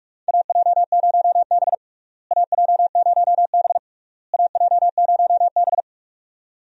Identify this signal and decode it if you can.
Morse code, heard as a keyed tone, about 43 wpm, 700 Hz. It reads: AJ0B AJ0B AJ0B
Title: USA Call Signs - 4 Characters - Repeated 3x - 45wpm